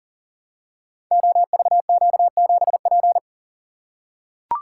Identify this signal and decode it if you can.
Morse code, heard as a keyed tone, about 40 wpm, 700 Hz. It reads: OVQ7P
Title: Code Groups - 5 Letter:Numbers - 40wpm